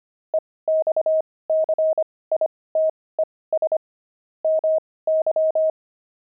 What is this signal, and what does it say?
Morse code, heard as a keyed tone, about 25 wpm, spaced as for 18 wpm, 630 Hz. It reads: EXCITES MY
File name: pangram-fs-18-cs-25-jazz